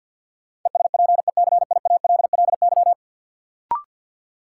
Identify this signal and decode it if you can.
Morse code, heard as a keyed tone, about 48 wpm, 700 Hz. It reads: ESPECIALLY
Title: Sentences from Top 700 Words - New Word Drill - 50wpm